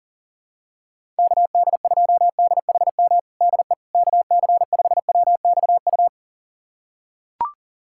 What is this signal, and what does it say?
Morse code, heard as a keyed tone, about 37 wpm, 700 Hz. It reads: KD2DHM DE KC5WXU
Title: USA Callsign DE Callsign - 6 Characters - 40wpm